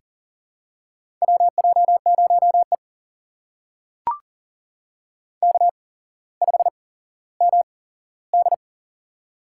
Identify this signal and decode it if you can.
Morse code, heard as a keyed tone, about 40 wpm, spaced as for 17 wpm, 700 Hz. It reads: WJ0E K5MD
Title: USA Call Signs - 4 Characters - 8x Character Spacing - 40wpm